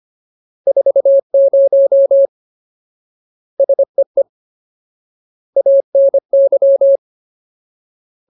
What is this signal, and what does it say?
Morse code, heard as a keyed tone, about 25 wpm, 550 Hz. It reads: 40 SEE ANY
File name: Ultimate Word List / 25WPM Ultimate